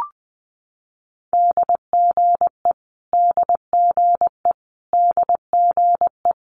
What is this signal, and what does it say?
Morse code, heard as a keyed tone, about 20 wpm, 700 Hz. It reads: DGE DGE DGE
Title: Multi-letter Phonemes Repeated 3x 20wpm